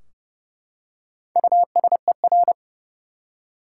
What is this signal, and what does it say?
Morse code, heard as a keyed tone, about 32 wpm, 700 Hz. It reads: USER
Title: Four Letter Words - 30wpm